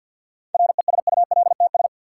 Morse code, heard as a keyed tone, about 53 wpm, 700 Hz. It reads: RESULTS
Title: Sets of 4 Words - New Word Drill - 50wpm